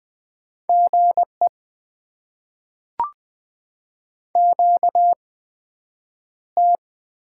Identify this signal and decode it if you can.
Morse code, heard as a keyed tone, about 20 wpm, spaced as for 9 wpm, 700 Hz. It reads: GE QT
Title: Top 20 QSO Elements - 8x Character Spacing - 20wpm